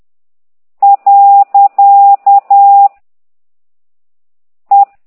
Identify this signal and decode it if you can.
Morse code, heard as a keyed tone, about 10 wpm, 800 Hz. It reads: . E
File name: exerc7